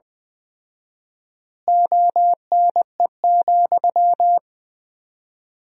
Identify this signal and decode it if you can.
Morse code, heard as a keyed tone, about 20 wpm, 700 Hz. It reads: ONE,
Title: 4x Word-Spacing - Sentences from Top 200 Words - 20wpm